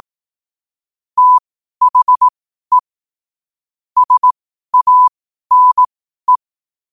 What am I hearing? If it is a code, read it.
Morse code, heard as a keyed tone, about 18 wpm, spaced as for 12 wpm, 1000 Hz. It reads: THE SANE
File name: qotdfiles / 12wpm